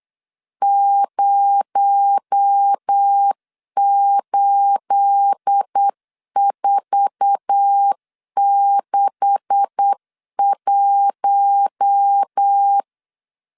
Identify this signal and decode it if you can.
Morse code, heard as a keyed tone, about 8 wpm, 800 Hz. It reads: 08461
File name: LEZIONI / 9 286 8wpm 3425